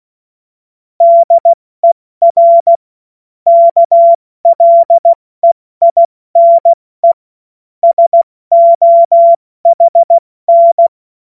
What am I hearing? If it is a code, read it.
Morse code, heard as a keyed tone, about 16 wpm, 680 Hz. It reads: DER KLEINE SOHN
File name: Morsekurs / Witze / Witz 14. Februar 2023 70 F80